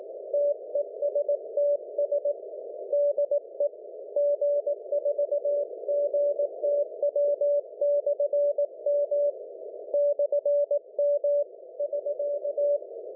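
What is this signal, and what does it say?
Morse code, heard as a keyed tone, about 18 wpm, 580 Hz. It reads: TESTS DE G4GTW/M /M <SK>
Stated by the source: CW on 472kHz MF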